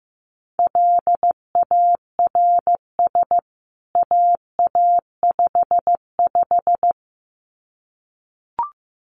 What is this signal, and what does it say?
Morse code, heard as a keyed tone, about 15 wpm, 700 Hz. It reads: LARS AA55
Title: CWT Exchange 15wpm